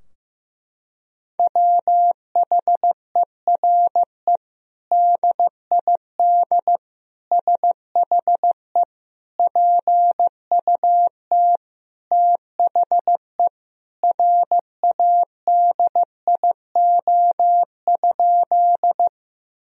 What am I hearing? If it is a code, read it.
Morse code, heard as a keyed tone, about 15 wpm, 700 Hz. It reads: WHERE DID SHE PUT THE RADIO?
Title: Sentences from Top 1000 Words - Review Part 3 - 15wpm